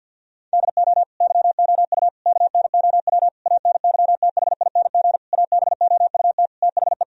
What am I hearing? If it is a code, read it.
Morse code, heard as a keyed tone, about 47 wpm, 700 Hz. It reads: DO YOU KNOW ANYTHING ABOUT THE
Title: Sentences from Top 1000 Words - Review Part 1 - 50wpm